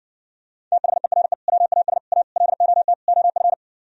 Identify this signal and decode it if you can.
Morse code, heard as a keyed tone, about 70 wpm, 700 Hz. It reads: THERE WAS A LOT OF